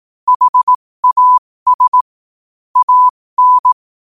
Morse code, heard as a keyed tone, about 18 wpm, spaced as for 15 wpm, 1000 Hz. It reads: HAS AN
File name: qotdfiles / 15wpm